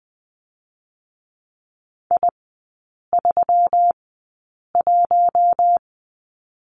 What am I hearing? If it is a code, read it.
Morse code, heard as a keyed tone, about 20 wpm, spaced as for 8 wpm, 700 Hz. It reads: I31
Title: 127 Lesson 21e - Number 6 - Three Characters 20wpm